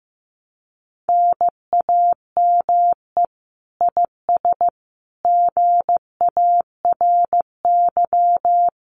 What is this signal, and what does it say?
Morse code, heard as a keyed tone, about 15 wpm, 700 Hz. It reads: NAME IS GARY
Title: Common QSO Phrases Encore 15wpm